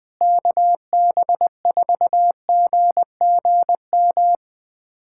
Morse code, heard as a keyed tone, about 20 wpm, 700 Hz. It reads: KB4GGM
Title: USA Callsign DE Callsign - 6 Characters - 20wpm